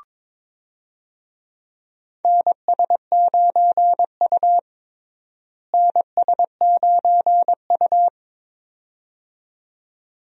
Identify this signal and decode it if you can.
Morse code, heard as a keyed tone, about 22 wpm, 700 Hz. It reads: NS9U NS9U
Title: USA Call Signs - 4 Characters - Repeated 2x - 22wpm